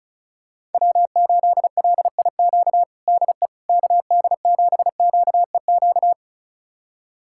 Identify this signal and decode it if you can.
Morse code, heard as a keyed tone, about 36 wpm, 700 Hz. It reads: W8LIQ DE KD7QEQ
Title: USA Callsign DE Callsign 35wpm